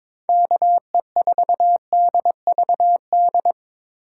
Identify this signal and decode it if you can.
Morse code, heard as a keyed tone, about 26 wpm, 700 Hz. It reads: KE4DVD
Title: Speed Racing - USA 2x3 Call Signs 22wpm